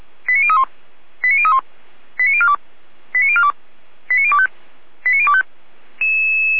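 Five-tone selective call (ZVEI), heard as quick runs of five tones, tones of about 70 ms. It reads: 89031 89031 89042 89042 89025 89025
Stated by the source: ZVEI five-tone call received via SDR, archived by the Feuerwehr Orlamünde fire brigade.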